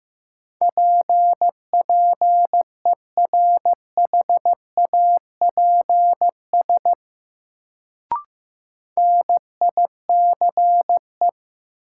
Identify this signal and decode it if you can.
Morse code, heard as a keyed tone, about 15 wpm, 700 Hz. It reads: PPERHAPS NICE
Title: Sentences from Top 1100 Words - New Word Drill - 15wpm